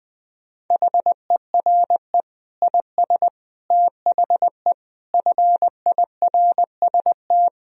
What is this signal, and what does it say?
Morse code, heard as a keyed tone, about 20 wpm, 700 Hz. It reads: HERE IS THE FIRST
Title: Sentences from Top 1000 Words - Review Part 2 - 20wpm